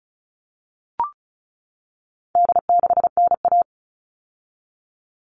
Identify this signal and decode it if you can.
Morse code, heard as a keyed tone, about 34 wpm, 700 Hz. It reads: D6NA
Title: All Cut Numbers - 4-Digit 35wpm